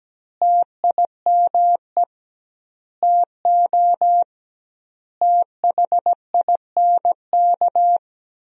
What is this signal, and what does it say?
Morse code, heard as a keyed tone, about 17 wpm, 700 Hz. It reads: TIME TO THINK
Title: 2x Word-Spacing - Sentences from Top 400 Words - 17wpm